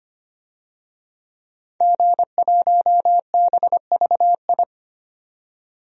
Morse code, heard as a keyed tone, about 25 wpm, 700 Hz. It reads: G1BVI
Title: DXCC-100 Call Signs - 5 Characters - 25wpm